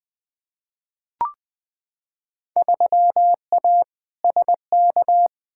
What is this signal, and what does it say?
Morse code, heard as a keyed tone, about 20 wpm, 700 Hz. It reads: 3A SK
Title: Field Day Exchanges Encore - 20wpm